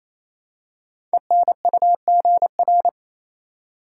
Morse code, heard as a keyed tone, about 27 wpm, 700 Hz. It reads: ENUGR
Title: Code Groups - 5 Letters - 28wpm